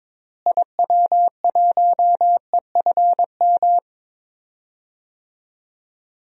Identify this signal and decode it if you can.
Morse code, heard as a keyed tone, about 22 wpm, 700 Hz. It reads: IW1EFM
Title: DXCC-100 Call Signs - 6 Characters - Repeated 2x - 22wpm